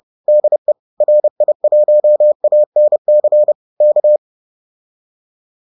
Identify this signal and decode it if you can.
Morse code, heard as a keyed tone, about 30 wpm, 600 Hz. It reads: DE RI1ANC K